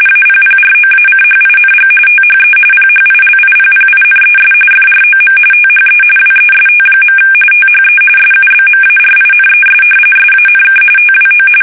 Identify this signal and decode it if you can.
RTTY, about 45.45 baud. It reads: APPROACHING THE PHILIPPINES FROM THE WEST. THE SUBMARINES SANK TWO
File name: fsk850 / battleshipnj